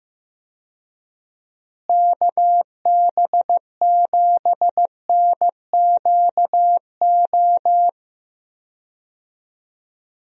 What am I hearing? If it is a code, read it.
Morse code, heard as a keyed tone, about 15 wpm, 700 Hz. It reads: KB7NQO
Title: USA Call Signs - 6 Characters - Repeated 3x - 15wpm